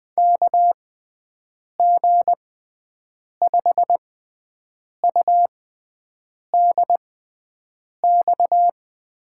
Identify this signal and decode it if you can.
Morse code, heard as a keyed tone, about 20 wpm, spaced as for 7 wpm, 700 Hz. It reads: KG5UDX
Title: USA Call Signs - 6 Characters - 6x Character Spacing - 20wpm